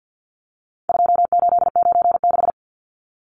Morse code, heard as a keyed tone, about 48 wpm, 700 Hz. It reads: 2896
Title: Numbers - 4-Digit 50wpm